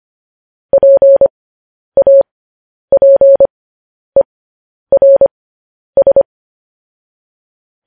Morse code, heard as a keyed tone, about 25 wpm, spaced as for 10 wpm, 550 Hz. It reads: PAPERS
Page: news-headlines-in-morse-code-at-10-wpm